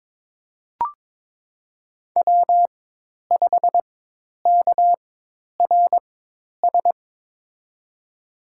Morse code, heard as a keyed tone, about 23 wpm, spaced as for 10 wpm, 700 Hz. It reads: W5KRS
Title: USA Call Signs - 5 Characters - 4x Character Spacing - 22wpm